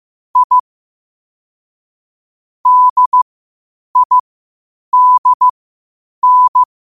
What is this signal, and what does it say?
Morse code, heard as a keyed tone, about 15 wpm, spaced as for 8 wpm, 1000 Hz. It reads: I DIDN
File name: qotdfiles / 8wpm